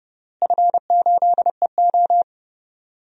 Morse code, heard as a keyed tone, about 30 wpm, 700 Hz. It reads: F8EO
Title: DXCC-100 Call Signs - 4 Characters - Repeated 3x - 30wpm